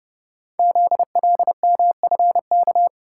Morse code, heard as a keyed tone, about 30 wpm, 700 Hz. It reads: ZLMFK
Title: Code Groups - 5 Letters - 30wpm